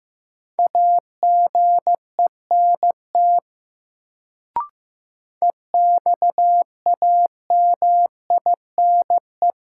Morse code, heard as a keyed tone, about 15 wpm, 700 Hz. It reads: AGENT EXAMINE
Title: Sentences from Top 1600 Words - New Word Drill - 15wpm